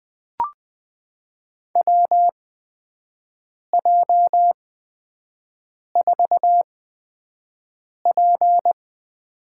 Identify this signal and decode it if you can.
Morse code, heard as a keyed tone, about 20 wpm, spaced as for 5 wpm, 700 Hz. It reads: WJ4P
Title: USA Call Signs - 4 Characters - 8x Character Spacing - 20wpm